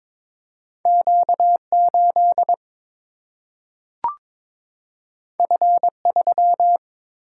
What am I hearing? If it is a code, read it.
Morse code, heard as a keyed tone, about 23 wpm, 700 Hz. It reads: Q 8 F 3
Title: Single-Letter Single-Number 22wpm